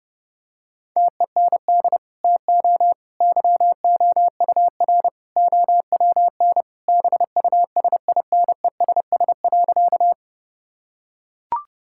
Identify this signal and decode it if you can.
Morse code, heard as a keyed tone, about 30 wpm, 700 Hz. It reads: TEND TO YOUR OWN BUSINESS.
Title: Sentences from Top 1000 Words - Review Part 5 - 30wpm